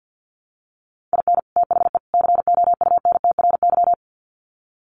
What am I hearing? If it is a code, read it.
Morse code, heard as a keyed tone, about 48 wpm, 700 Hz. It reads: IN THE COUNTRY
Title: Sets of 3 Words 50wpm